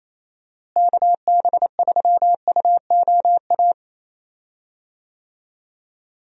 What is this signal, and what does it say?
Morse code, heard as a keyed tone, about 27 wpm, 700 Hz. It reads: KB3UOA
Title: USA Call Signs - 6 Characters - Repeated 2x - 28wpm